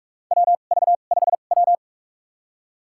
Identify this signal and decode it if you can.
Morse code, heard as a keyed tone, about 47 wpm, spaced as for 33 wpm, 700 Hz. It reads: WV5W
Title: USA Call Signs - 4 Characters - 2x Character Spacing - 45wpm